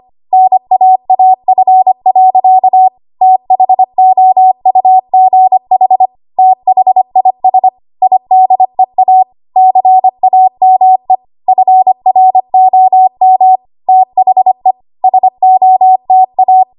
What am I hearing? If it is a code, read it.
Morse code, heard as a keyed tone, about 25 wpm, 750 Hz. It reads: NAAF. THOUGH THIS IDEA CAME FROM THE SOTA
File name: Morse / 25WPM